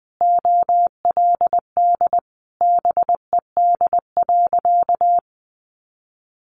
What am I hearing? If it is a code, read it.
Morse code, heard as a keyed tone, about 20 wpm, 700 Hz. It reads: OLD BED.
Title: Sentences from Top 500 Words 20wpm